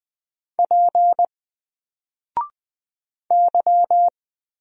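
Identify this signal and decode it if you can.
Morse code, heard as a keyed tone, about 20 wpm, 700 Hz. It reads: P Y
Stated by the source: Single letter and number Morse drill at 20 wpm, from N4LCD.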